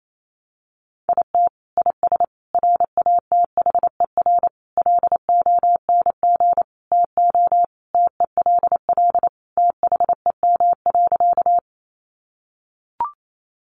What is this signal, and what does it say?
Morse code, heard as a keyed tone, about 28 wpm, 700 Hz. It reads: IT IS RATHER LONG TO TELL THEM.
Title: Sentences from Top 700 Words 28wpm